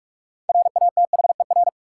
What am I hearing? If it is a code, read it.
Morse code, heard as a keyed tone, about 43 wpm, 700 Hz. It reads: RATHER
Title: Sets of 2 Words - New Word Drill - 45wpm